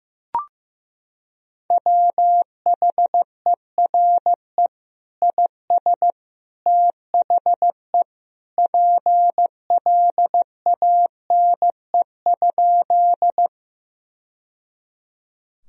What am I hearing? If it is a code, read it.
Morse code, heard as a keyed tone, about 15 wpm, 700 Hz. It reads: WHERE IS THE PLANE?
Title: Sentences from Top 1000 Words - Review Part 3 - 15wpm